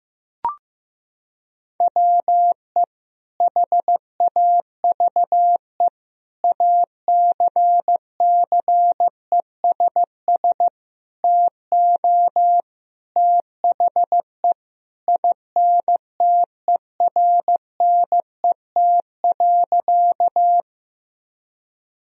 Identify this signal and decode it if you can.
Morse code, heard as a keyed tone, about 15 wpm, 700 Hz. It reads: WE HAVE ACCESS TO THE INTERNET.
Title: Sentences from Top 1000 Words - Review Part 4 - 15wpm